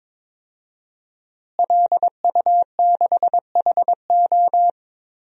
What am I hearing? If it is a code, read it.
Morse code, heard as a keyed tone, about 22 wpm, 700 Hz. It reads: LU6HO
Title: DXCC-100 Call Signs - 5 Characters - Repeated 3x - 22wpm